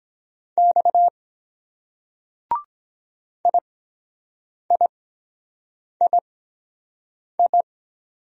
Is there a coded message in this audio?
Yes: Morse code from 0.6 to 7.6 s, heard as a keyed tone, about 23 wpm, spaced as for 7 wpm, 700 Hz. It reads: X IIII